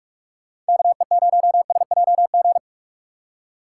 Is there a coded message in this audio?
Yes: Morse code from 0.7 to 2.6 s, heard as a keyed tone, about 59 wpm, 700 Hz. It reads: KE0SJG